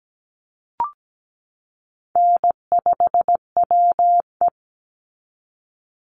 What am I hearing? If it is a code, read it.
Morse code, heard as a keyed tone, about 17 wpm, 700 Hz. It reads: N5WE
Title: Call Signs Repeated 2x - 17wpm